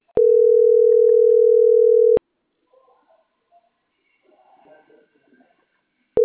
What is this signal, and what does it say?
Ringback tone (North America), 440 + 480 Hz.